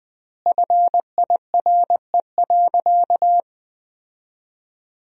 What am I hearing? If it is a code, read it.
Morse code, heard as a keyed tone, about 21 wpm, 700 Hz. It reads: FIRE.
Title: 4x Word-Spacing - Sentences from Top 400 Words - 20wpm